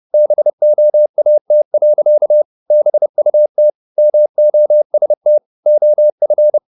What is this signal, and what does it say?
Morse code, heard as a keyed tone, about 30 wpm, 600 Hz. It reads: BOAT. BUT MOST OF
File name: the old man and the sea / the old man and the sea 1 30wpm